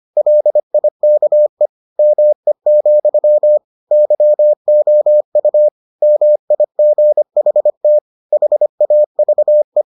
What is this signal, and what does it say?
Morse code, heard as a keyed tone, about 25 wpm, 600 Hz. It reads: LIKE ME, YOU MIGHT HAVE